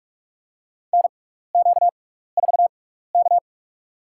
Morse code, heard as a keyed tone, about 44 wpm, spaced as for 15 wpm, 700 Hz. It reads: NQ4K
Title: USA Call Signs - 4 Characters - 6x Character Spacing - 45wpm